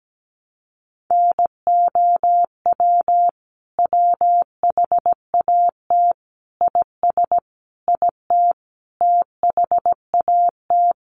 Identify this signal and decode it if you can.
Morse code, heard as a keyed tone, about 17 wpm, 700 Hz. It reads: NOW WHAT IS IT THAT
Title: Sentences from Top 200 Words 17wpm